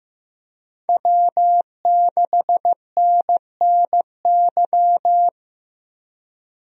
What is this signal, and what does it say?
Morse code, heard as a keyed tone, about 15 wpm, 700 Hz. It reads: W6NNY
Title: USA Call Signs - 5 Characters - Repeated 3x - 15wpm